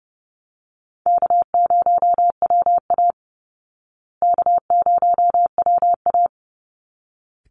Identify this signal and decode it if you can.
Morse code, heard as a keyed tone, about 30 wpm, 700 Hz. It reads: K0WA K0WA
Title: Call Signs Repeated 3x - 30wpm